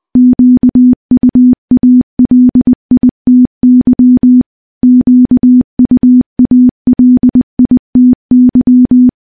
Morse code, heard as a keyed tone, about 20 wpm, 250 Hz. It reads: QUALITY QUALITY
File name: morse-training-20WPM